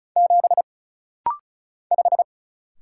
Morse code, heard as a keyed tone, about 34 wpm, 700 Hz. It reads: 7 5
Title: Single Letter-Number - Rapid-Fire 35wpm